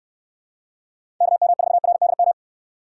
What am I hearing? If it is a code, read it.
Morse code, heard as a keyed tone, about 63 wpm, 700 Hz. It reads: KM4RDM